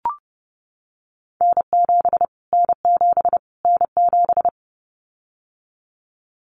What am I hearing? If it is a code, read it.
Morse code, heard as a keyed tone, about 30 wpm, 700 Hz. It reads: N7 N7 N7
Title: Callsign Prefixes for Top 25 DXCC - Repeated 3x - 30wpm